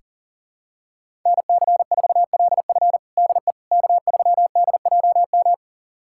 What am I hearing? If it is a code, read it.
Morse code, heard as a keyed tone, about 37 wpm, 700 Hz. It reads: NC4LF DE K3DJM
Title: USA Callsign DE Callsign - 5 Characters - 40wpm